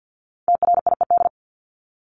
Morse code, heard as a keyed tone, about 50 wpm, 700 Hz. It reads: TRIED
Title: ED Suffix - Top 100 Words with Base Words 50wpm